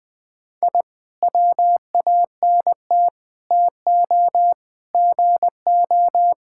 Morse code, heard as a keyed tone, about 20 wpm, 700 Hz. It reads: I WANT TO GO
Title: Sentences with 5-Word Phrases - 20wpm